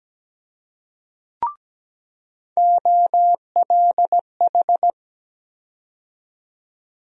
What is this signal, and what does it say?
Morse code, heard as a keyed tone, about 17 wpm, 700 Hz. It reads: OLH
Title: USA Call Sign Postfixes - 3 Letters 17wpm